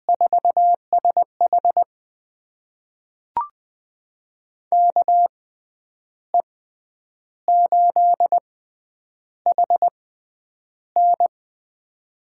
Morse code, heard as a keyed tone, about 20 wpm, spaced as for 9 wpm, 700 Hz. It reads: 4SH KE8HN
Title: USA Call Signs - 5 Characters - 6x Character Spacing - 20wpm